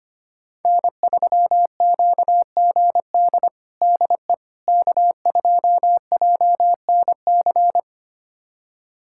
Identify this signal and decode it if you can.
Morse code, heard as a keyed tone, about 25 wpm, 700 Hz. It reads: N3QGD DE K2JNC
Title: USA Callsign DE Callsign - 5 Characters - 25wpm